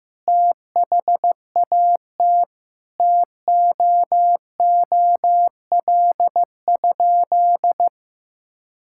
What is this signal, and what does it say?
Morse code, heard as a keyed tone, about 15 wpm, 700 Hz. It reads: THAT TOOL?
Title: Sentences from Top 1000 Words - Review Part 6 - 15wpm